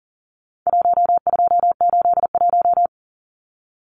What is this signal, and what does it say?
Morse code, heard as a keyed tone, about 40 wpm, 700 Hz. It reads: 1281
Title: Numbers - 4-Digit 40wpm